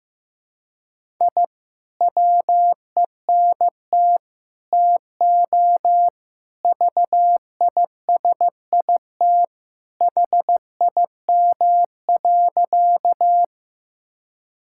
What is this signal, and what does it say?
Morse code, heard as a keyed tone, about 14 wpm, 700 Hz. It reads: I WENT TO VISIT HIM.
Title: Sentences from Top 1000 Words - Review Part 6 - 15wpm